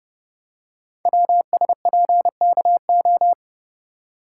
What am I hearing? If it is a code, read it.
Morse code, heard as a keyed tone, about 29 wpm, 700 Hz. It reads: WSPKO
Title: Code Groups - 5 Letters - 30wpm